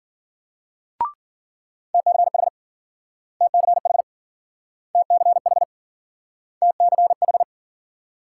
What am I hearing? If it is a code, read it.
Morse code, heard as a keyed tone, about 49 wpm, 700 Hz. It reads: TCH TCH TCH TCH
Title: Speed-Racing Multi-letter Phonemes - 40wpm